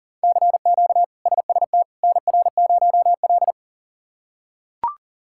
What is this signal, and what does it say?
Morse code, heard as a keyed tone, about 40 wpm, 700 Hz. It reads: CQ SST NR0L
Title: K1USN SST QSO Practice - 40wpm